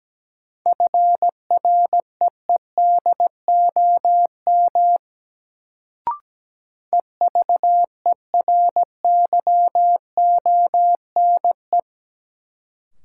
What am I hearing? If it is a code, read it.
Morse code, heard as a keyed tone, about 17 wpm, 700 Hz. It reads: FREEDOM EVERYONE
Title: Sentences from Top 800 Words - New Word Drill - 17wpm